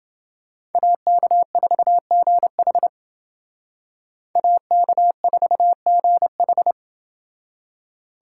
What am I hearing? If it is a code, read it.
Morse code, heard as a keyed tone, about 28 wpm, 700 Hz. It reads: AK4GH AK4GH